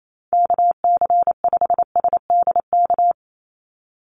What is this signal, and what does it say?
Morse code, heard as a keyed tone, about 28 wpm, 700 Hz. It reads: KC5SDK
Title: USA Call Signs 28wpm